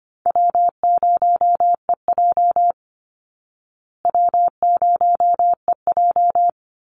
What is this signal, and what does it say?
Morse code, heard as a keyed tone, about 25 wpm, 700 Hz. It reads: W0EJ W0EJ
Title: Easy Call Signs Repeated 3x - 25wpm